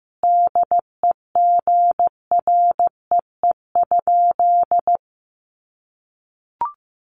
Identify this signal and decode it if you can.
Morse code, heard as a keyed tone, about 15 wpm, 700 Hz. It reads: DEGREE?
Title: Sentences from Top 1500 Words 15wpm